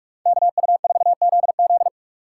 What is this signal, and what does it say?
Morse code, heard as a keyed tone, about 40 wpm, 700 Hz. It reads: KU4ZZ